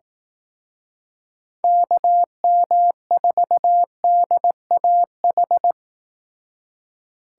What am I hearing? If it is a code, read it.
Morse code, heard as a keyed tone, about 18 wpm, 700 Hz. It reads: KM4DAH
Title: Speed Racing - USA 2x3 Call Signs 15wpm